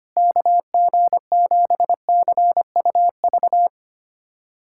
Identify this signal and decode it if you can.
Morse code, heard as a keyed tone, about 25 wpm, 700 Hz. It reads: KG7CUV
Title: USA Callsign DE Callsign - 6 Characters - 25wpm